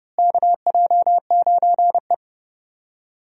Morse code, heard as a keyed tone, about 30 wpm, 700 Hz. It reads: KJ9E
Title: USA Call Signs - 4 Characters - Repeated 3x - 30wpm